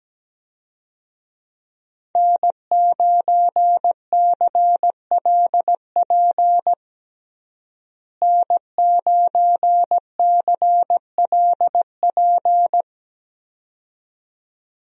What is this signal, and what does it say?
Morse code, heard as a keyed tone, about 17 wpm, 700 Hz. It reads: N9CLP N9CLP
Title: USA Call Signs - 5 Characters - Repeated 2x - 17wpm